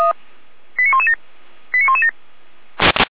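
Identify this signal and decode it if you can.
Five-tone selective call (ZVEI), heard as quick runs of five tones, tones of about 70 ms. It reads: 89117 89117
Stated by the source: ZVEI five-tone selective call 89080, received on an SDR.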